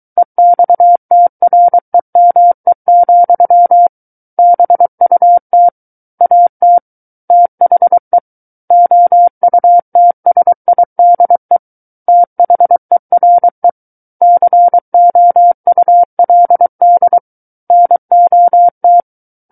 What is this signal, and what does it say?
Morse code, heard as a keyed tone, about 23 wpm, 700 Hz. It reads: EXTREME, BUT AT THE OUTSIDE THERE COULD NOT
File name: Book One / 14-Chapter 14 - 23 WPM